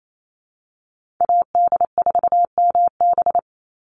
Morse code, heard as a keyed tone, about 28 wpm, 700 Hz. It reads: AD4MB
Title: USA Call Signs 28wpm